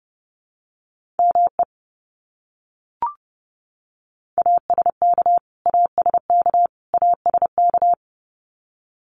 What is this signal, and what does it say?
Morse code, heard as a keyed tone, about 30 wpm, 700 Hz. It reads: ME ASK ASK ASK
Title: Top 200 Words Repeated 30wpm